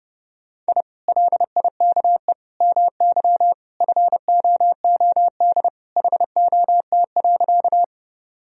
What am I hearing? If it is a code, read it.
Morse code, heard as a keyed tone, about 30 wpm, 700 Hz. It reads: I LIKE MY FOOD HOT.
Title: Sentences from Top 1000 Words - Review Part 3 - 30wpm